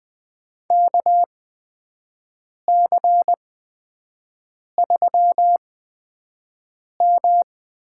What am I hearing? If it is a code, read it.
Morse code, heard as a keyed tone, about 20 wpm, spaced as for 5 wpm, 700 Hz. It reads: KC3M
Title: USA Call Signs - 6 Characters - 8x Character Spacing - 20wpm